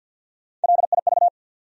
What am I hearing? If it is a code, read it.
Morse code, heard as a keyed tone, about 52 wpm, 700 Hz. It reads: LIV
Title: USA Common Names - Max 5 Letters - 50wpm